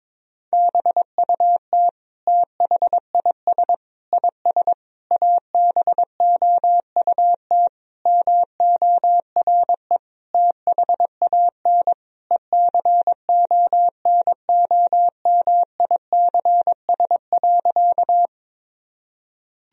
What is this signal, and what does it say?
Morse code, heard as a keyed tone, about 22 wpm, 700 Hz. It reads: BUT THIS IS ABOUT MORE THAN ECONOMICS.
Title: Sentences from Top 1000 Words - Review Part 3 - 22wpm